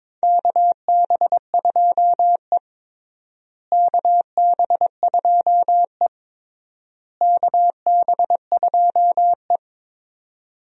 Morse code, heard as a keyed tone, about 22 wpm, 700 Hz. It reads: KB2E KB2E KB2E
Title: USA Call Signs - 4 Characters - Repeated 3x - 22wpm